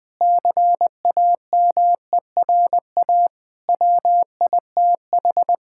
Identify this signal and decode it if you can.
Morse code, heard as a keyed tone, about 20 wpm, 700 Hz. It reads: CAMERA WITH
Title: Sentences from Top 1000 Words - Review Part 5 - 20wpm